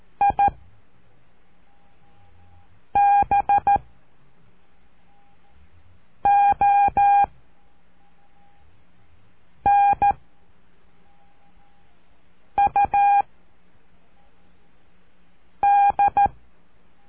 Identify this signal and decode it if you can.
Morse code, heard as a keyed tone, about 13 wpm, 810 Hz. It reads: I B O N U D
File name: cursos / CURSO DE TELEGRAFIA / Telegrafia - Trilha 04